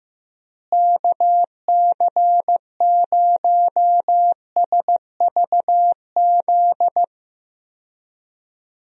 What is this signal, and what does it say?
Morse code, heard as a keyed tone, about 15 wpm, 700 Hz. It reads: KC0SVZ
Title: USA Call Signs - 6 Characters - Repeated 3x - 15wpm